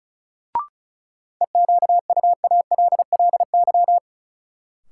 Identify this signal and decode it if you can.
Morse code, heard as a keyed tone, about 36 wpm, 700 Hz. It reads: EQUALLY
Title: Sentences from Top 1400 Words - New Word Drill Alt - 35wpm